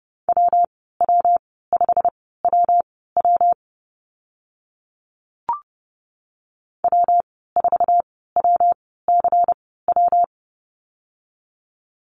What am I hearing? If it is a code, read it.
Morse code, heard as a keyed tone, about 30 wpm, spaced as for 17 wpm, 700 Hz. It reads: WW5WW W4WCW
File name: Lesson 19g - Letter w - Call Signs 30wpm